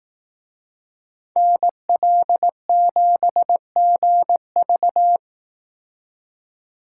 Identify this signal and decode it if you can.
Morse code, heard as a keyed tone, about 18 wpm, 700 Hz. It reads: NL7GV
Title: Speed Racing - USA 2x2 Call Signs 15wpm